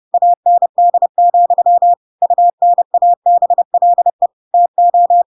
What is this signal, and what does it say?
Morse code, heard as a keyed tone, about 30 wpm, 700 Hz. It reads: AND, UNABLE TO
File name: Pinocchio / Pinocchio ch8 30 30 ik1hge